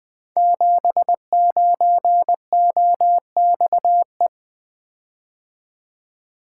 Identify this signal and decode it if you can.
Morse code, heard as a keyed tone, about 20 wpm, 700 Hz. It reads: 79OXE
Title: Code Groups - 5 Letter:Numbers - 20wpm